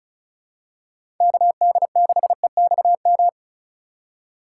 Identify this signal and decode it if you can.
Morse code, heard as a keyed tone, about 34 wpm, 700 Hz. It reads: KD6EXM
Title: USA Call Signs - 4x Character Spacing - 35wpm